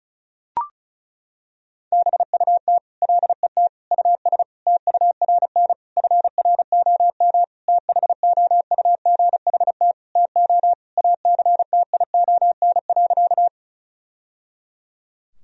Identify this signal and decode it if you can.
Morse code, heard as a keyed tone, about 34 wpm, 700 Hz. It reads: BUT LET US TURN FROM THOUGHT TO ACTION.
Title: Sentences from Top 1000 Words - Review Part 4 - 35wpm